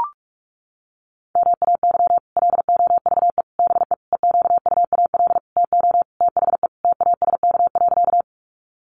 Morse code, heard as a keyed tone, about 44 wpm, 700 Hz. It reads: MAY LOVE BE EQUAL TO THE TASK.
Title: Sentences from Top 1400 Words 45wpm